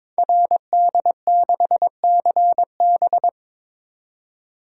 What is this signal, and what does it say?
Morse code, heard as a keyed tone, about 22 wpm, 700 Hz. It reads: RD6CB
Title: Code Groups - 5 Letter:Numbers - 22wpm